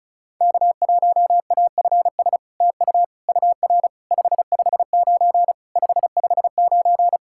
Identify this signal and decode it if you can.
Morse code, heard as a keyed tone, about 35 wpm, 700 Hz. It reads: K1AFS TU UR 559 559
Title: SOTA QSO Practice 35wpm